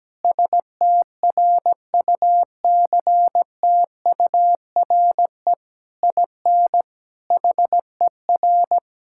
Morse code, heard as a keyed tone, about 17 wpm, 700 Hz. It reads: STRUCTURE IN HER
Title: Sentences from Top 1000 Words - Review Part 6 - 17wpm